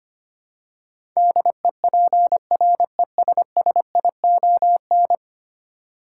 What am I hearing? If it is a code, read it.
Morse code, heard as a keyed tone, about 25 wpm, 700 Hz. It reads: DEPRESSION
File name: 10 Letter Words - First 3 Letters Given - 25wpm